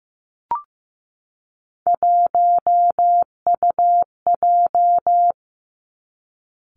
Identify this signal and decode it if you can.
Morse code, heard as a keyed tone, about 15 wpm, 700 Hz. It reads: 1UJ
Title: Three Challenging Letter-Number Combinations 15wpm